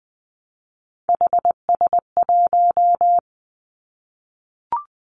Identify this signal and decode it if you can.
Morse code, heard as a keyed tone, about 20 wpm, 700 Hz. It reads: HS1
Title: Three Challenging Letter-Number Combinations 20wpm